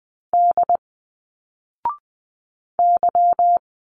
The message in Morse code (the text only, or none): D Y